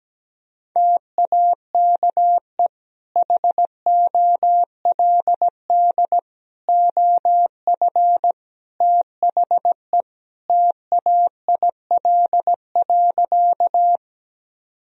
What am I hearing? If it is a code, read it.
Morse code, heard as a keyed tone, about 17 wpm, 700 Hz. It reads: TAKE HOLD OF THE TAIL.
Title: Sentences from Top 1000 Words - Review Part 3 - 17wpm